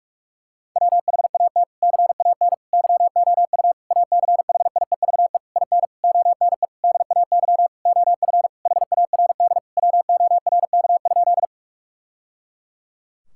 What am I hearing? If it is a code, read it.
Morse code, heard as a keyed tone, about 46 wpm, 700 Hz. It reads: WHAT CAN YOU ACHIEVE IN ONE DAY OF HARD WORK?
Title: Sentences from Top 1000 Words - Review Part 6 - 45wpm